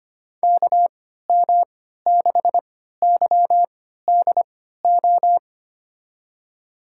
Morse code, heard as a keyed tone, about 25 wpm, spaced as for 14 wpm, 700 Hz. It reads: KM6YDO
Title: USA Call Signs - 6 Characters - 3x Character Spacing - 25wpm